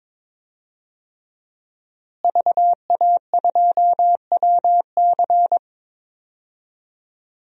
Morse code, heard as a keyed tone, about 22 wpm, 700 Hz. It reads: VA2WC
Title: DXCC-100 Call Signs - 5 Characters - Repeated 3x - 22wpm